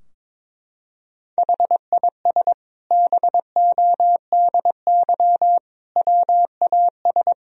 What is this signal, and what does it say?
Morse code, heard as a keyed tone, about 22 wpm, 700 Hz. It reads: HIS BODY WAS
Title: Sentences from Top 1000 Words - Review Part 2 - 22wpm